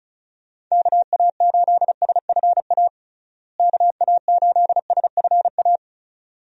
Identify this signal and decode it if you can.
Morse code, heard as a keyed tone, about 36 wpm, 700 Hz. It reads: KA8SFA KA8SFA
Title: USA Call Signs - 6 Characters - Repeated 2x - 35wpm